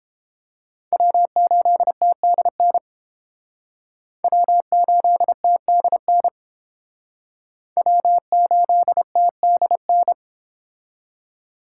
Morse code, heard as a keyed tone, about 30 wpm, 700 Hz. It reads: W8TDN W8TDN W8TDN